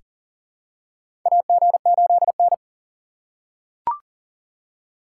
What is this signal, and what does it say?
Morse code, heard as a keyed tone, about 40 wpm, 700 Hz. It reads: AG8N
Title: USA Call Signs - 4 Characters - 6x Character Spacing - 40wpm